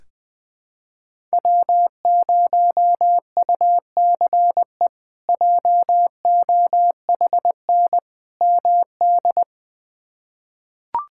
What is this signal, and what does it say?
Morse code, heard as a keyed tone, about 20 wpm, 700 Hz. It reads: W0UCE JOHN MD
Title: K1USN SST QSO Practice - 20wpm